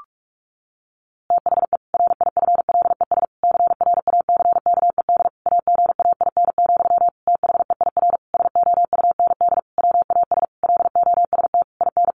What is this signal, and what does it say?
Morse code, heard as a keyed tone, about 44 wpm, 700 Hz. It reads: THE RIFLES CRACKED AGAIN, THEIR SOUND WAS LOST IN
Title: A Sound Of Thunder 45wpm